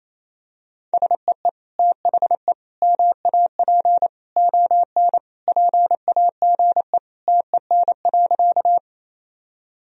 Morse code, heard as a keyed tone, about 28 wpm, 700 Hz. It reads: SEE THE MAP ON PAGE TEN.
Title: Sentences from Top 1000 Words - Review Part 2 - 28wpm